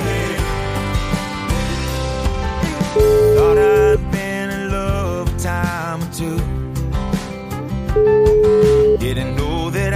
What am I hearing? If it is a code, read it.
Ringback tone (Europe, 425 Hz), with noise.